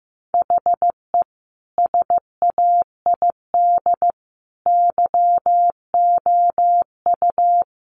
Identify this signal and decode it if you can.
Morse code, heard as a keyed tone, about 15 wpm, 700 Hz. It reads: HE SAID YOU
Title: Sentences from Top 1300 Words 15wpm